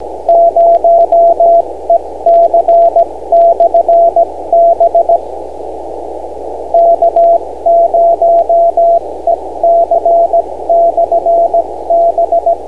Morse code, heard as a keyed tone, about 17 wpm, 680 Hz, with radio noise. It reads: 0EC/B K0EC/B
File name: beacons / k0ec-b